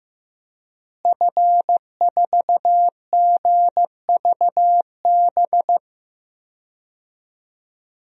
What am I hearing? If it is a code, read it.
Morse code, heard as a keyed tone, about 15 wpm, 700 Hz. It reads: F4GVB
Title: DXCC-100 Call Signs - 5 Characters - Repeated 3x - 15wpm